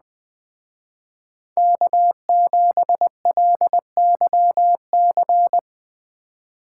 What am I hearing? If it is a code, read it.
Morse code, heard as a keyed tone, about 20 wpm, 700 Hz. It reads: K7LYC